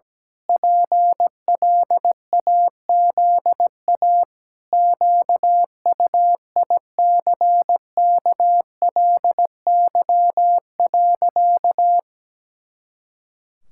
Morse code, heard as a keyed tone, about 17 wpm, 700 Hz. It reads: PLAZA QUICKLY.
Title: Pangram Challenge v2 - 17wpm